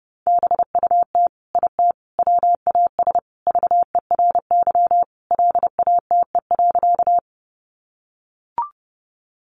Morse code, heard as a keyed tone, about 30 wpm, 700 Hz. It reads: BUT IT WAS VERY LATE.
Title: Sentences from Top 300 Words 30wpm